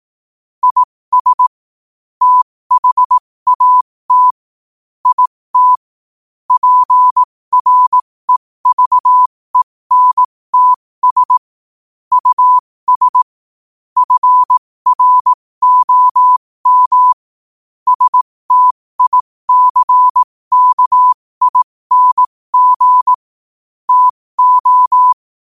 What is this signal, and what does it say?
Morse code, heard as a keyed tone, about 18 wpm, spaced as for 15 wpm, 1000 Hz. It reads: IS THAT IT PREVENTS US FROM STICKING TO
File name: qotdfiles / 15wpm